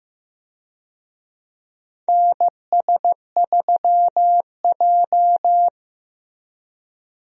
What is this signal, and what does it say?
Morse code, heard as a keyed tone, about 15 wpm, 700 Hz. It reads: NS3J
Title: USA Call Signs - 4 Characters - Repeated 3x - 15wpm